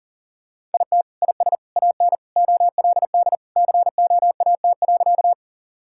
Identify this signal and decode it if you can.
Morse code, heard as a keyed tone, about 40 wpm, 700 Hz. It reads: IT IS AN OLD COAT.
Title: Sentences from Top 1600 Words 40wpm